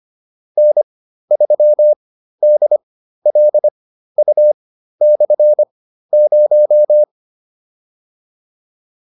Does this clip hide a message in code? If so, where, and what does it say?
Yes: Morse code from 0.6 to 7.0 s, heard as a keyed tone, about 25 wpm, spaced as for 13 wpm, 600 Hz. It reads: N3DLU/0